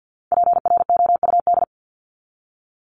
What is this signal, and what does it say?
Morse code, heard as a keyed tone, about 48 wpm, 700 Hz. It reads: PROUD
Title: Top 1501-1750 Words 50wpm